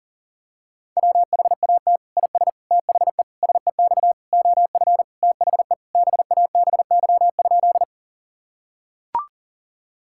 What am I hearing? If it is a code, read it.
Morse code, heard as a keyed tone, about 40 wpm, 700 Hz. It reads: WHAT IS THE SEX OF THE BABY?
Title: Sentences from Top 1600 Words 40wpm